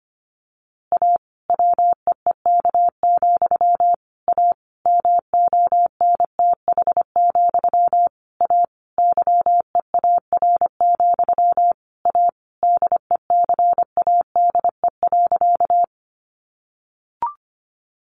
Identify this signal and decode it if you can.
Morse code, heard as a keyed tone, about 25 wpm, 700 Hz. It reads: A WEEK, A MONTH, A YEAR, A DECADE.
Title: A Sound Of Thunder 25wpm